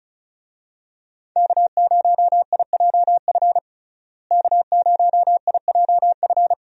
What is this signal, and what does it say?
Morse code, heard as a keyed tone, about 34 wpm, 700 Hz. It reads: K0IJF K0IJF
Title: USA Call Signs - 5 Characters - Repeated 3x - 35wpm